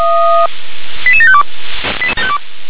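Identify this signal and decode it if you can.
Five-tone selective call (ZVEI), heard as quick runs of five tones, tones of about 70 ms. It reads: 88752
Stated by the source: ZVEI five-tone selective call received by SDR.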